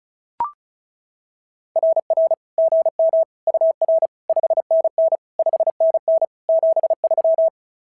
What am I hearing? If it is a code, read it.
Morse code, heard as a keyed tone, about 33 wpm, 660 Hz. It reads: RR GM UR 5NN 5NN 73
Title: SOTA QSO Practice 35wpm